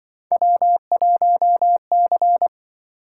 Morse code, heard as a keyed tone, about 24 wpm, 700 Hz. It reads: W1C
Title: Speed-Racing - Three Letter-Number Combinations - 20wpm